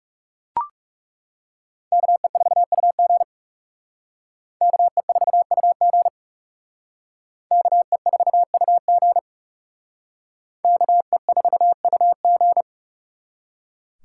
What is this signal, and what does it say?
Morse code, heard as a keyed tone, about 36 wpm, 700 Hz. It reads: KE4UG KE4UG KE4UG KE4UG
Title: Speed Racing - USA 2x2 Call Signs 30wpm